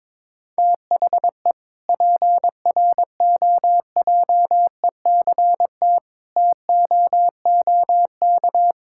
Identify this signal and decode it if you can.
Morse code, heard as a keyed tone, about 22 wpm, 700 Hz. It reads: THE PROJECT TOOK
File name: Sentences with 7-Word Phrases - 22wpm